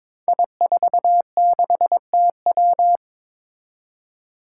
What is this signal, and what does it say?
Morse code, heard as a keyed tone, about 23 wpm, 700 Hz. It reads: I46TW
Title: Five Challenging Letter-Number Combinations 22wpm